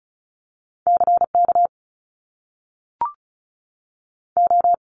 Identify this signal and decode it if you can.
Morse code, heard as a keyed tone, about 36 wpm, 700 Hz. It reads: CK O
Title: Top Two Letter Combinations in Words 35wpm